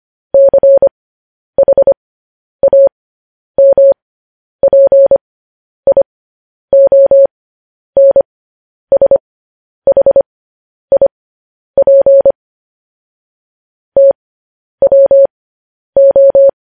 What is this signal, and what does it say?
Morse code, heard as a keyed tone, about 25 wpm, spaced as for 10 wpm, 550 Hz. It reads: CHAMPIONSHIP TWO
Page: news-headlines-in-morse-code-at-10-wpm